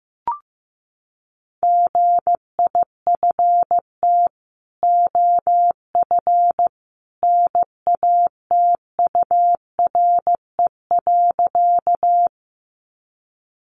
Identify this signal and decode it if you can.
Morse code, heard as a keyed tone, about 15 wpm, 700 Hz. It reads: GIFT OF NATURE.
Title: Sentences from Top 900 Words 15wpm